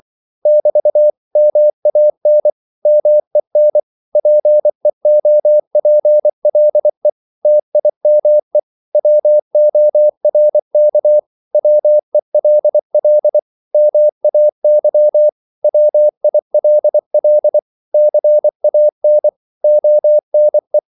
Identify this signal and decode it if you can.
Morse code, heard as a keyed tone, about 24 wpm, 600 Hz. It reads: <BT> MAN MEN PEOPLE TIME WORK WELL MAY WILL CAN ONE
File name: CW / Practice / 500MostCommonEnglishWords24